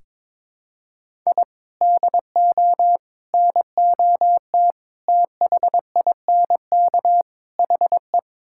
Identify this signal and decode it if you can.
Morse code, heard as a keyed tone, about 21 wpm, 700 Hz. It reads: I DO NOT THINK HE
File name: practice / 22wpm / Build A Phrase - 5-Word Phrases - 22wpm